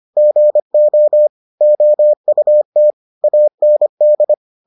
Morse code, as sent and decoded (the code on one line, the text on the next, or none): --. --- / --- ..- - / .- -. -..
GO OUT AND